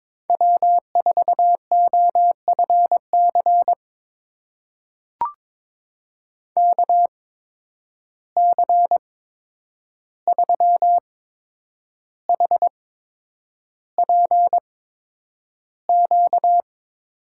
Morse code, heard as a keyed tone, about 22 wpm, spaced as for 9 wpm, 700 Hz. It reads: W4OFC KC3HPQ
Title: USA Call Signs - 8x Character Spacing - 22wpm